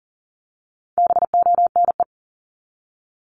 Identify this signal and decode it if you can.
Morse code, heard as a keyed tone, about 40 wpm, 700 Hz. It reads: BONE
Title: Top 1501-1750 Words 40wpm